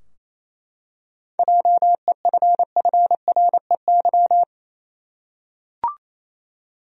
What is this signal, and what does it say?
Morse code, heard as a keyed tone, about 28 wpm, 700 Hz. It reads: JEFFREY
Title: USA Common Names - 28wpm